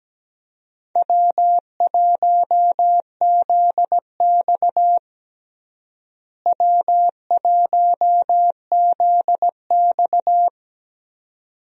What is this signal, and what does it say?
Morse code, heard as a keyed tone, about 17 wpm, 700 Hz. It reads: W1ZX W1ZX
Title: USA Call Signs - 4 Characters - Repeated 2x - 17wpm